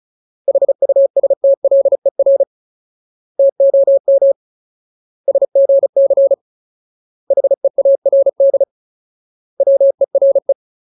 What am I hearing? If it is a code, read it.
Morse code, heard as a keyed tone, about 34 wpm, 550 Hz. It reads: HUSTLER TOM SGC HEARD WERE
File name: Ultimate Word List / 35WPM Ultimate